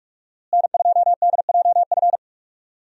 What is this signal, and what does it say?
Morse code, heard as a keyed tone, about 45 wpm, 700 Hz. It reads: N2DJF
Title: USA Call Signs - 5 Characters - Repeated 3x - 45wpm